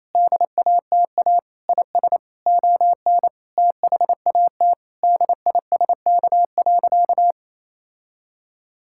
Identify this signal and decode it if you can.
Morse code, heard as a keyed tone, about 28 wpm, 700 Hz. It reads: DATA IS ON THAT DISK.
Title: Sentences from Top 1000 Words - Review Part 6 - 28wpm